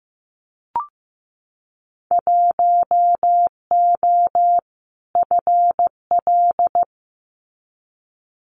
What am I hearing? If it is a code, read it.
Morse code, heard as a keyed tone, about 15 wpm, 700 Hz. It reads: 1O FL
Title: Winter Field Day Exchanges 15wpm